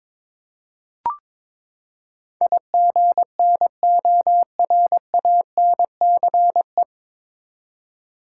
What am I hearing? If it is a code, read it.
Morse code, heard as a keyed tone, about 22 wpm, 700 Hz. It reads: IGNORANCE
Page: Morse Code Ninja practice 22wpm